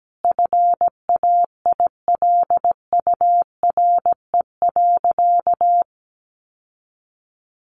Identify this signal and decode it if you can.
Morse code, heard as a keyed tone, about 17 wpm, 700 Hz. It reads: FAILURE.
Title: Sentences from Top 900 Words 17wpm